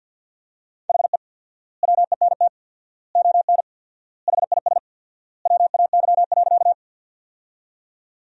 Morse code, heard as a keyed tone, about 50 wpm, 700 Hz. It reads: HE WENT ON HIS WAY.
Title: 4x Word-Spacing - Sentences from Top 200 Words - 50wpm